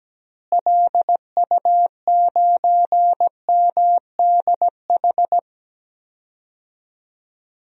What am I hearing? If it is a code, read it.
Morse code, heard as a keyed tone, about 17 wpm, 700 Hz. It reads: LU9MDH
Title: DXCC-100 Call Signs - 6 Characters - Repeated 2x - 17wpm